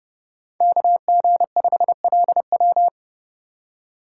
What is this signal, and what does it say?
Morse code, heard as a keyed tone, about 30 wpm, 700 Hz. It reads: KG5LW
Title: USA Call Signs - 5 Characters - 2x Character Spacing - 30wpm